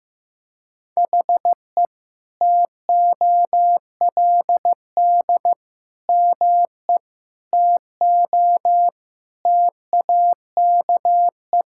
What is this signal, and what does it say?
Morse code, heard as a keyed tone, about 15 wpm, 700 Hz. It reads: HE TOLD ME TO TAKE
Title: Sentences from Top 1000 Words - Review Part 6 - 15wpm